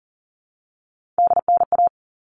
Morse code, heard as a keyed tone, about 40 wpm, 700 Hz. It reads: DNA
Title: All Cut Numbers - 3-Digit 40wpm